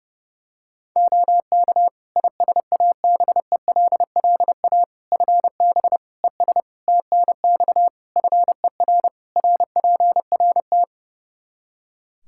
Morse code, heard as a keyed tone, about 30 wpm, 700 Hz. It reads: OK ISABELLA FB ES TNX FER RPRT
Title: Practice QSO - Exchange 2 - Info - 30wpm